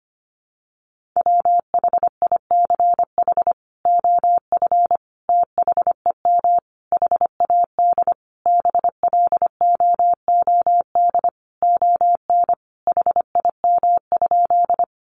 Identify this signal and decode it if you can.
Morse code, heard as a keyed tone, about 25 wpm, 700 Hz. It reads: WHICH OF THEM HAD BLOOD ON HIM?
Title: Sentences from Top 1100 Words 25wpm